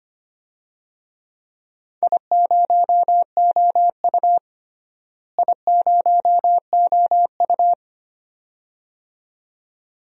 Morse code, heard as a keyed tone, about 25 wpm, 700 Hz. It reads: I0OU I0OU
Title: DXCC-100 Call Signs - 4 Characters - Repeated 2x - 25wpm